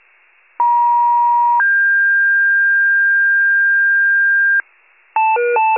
Two-tone page, heard as a one-second tone, then a three-second tone, about 950 then 1600 Hz, with noise. It is heard from 0.6 to 4.6 s.